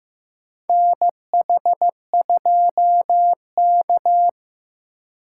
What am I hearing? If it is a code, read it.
Morse code, heard as a keyed tone, about 16 wpm, 700 Hz. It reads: NH2K
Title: USA Call Signs - 4 Characters - 4x Character Spacing - 15wpm